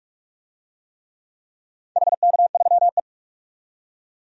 Morse code, heard as a keyed tone, about 46 wpm, 700 Hz. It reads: HK3E
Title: DXCC-100 Call Signs - 4 Characters - 45wpm